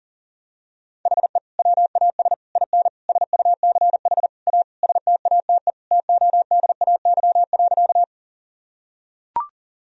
Morse code, heard as a keyed tone, about 40 wpm, 700 Hz. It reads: HE WAS IN SUCH A STATE TODAY.
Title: Sentences from Top 1000 Words - Review Part 2 - 40wpm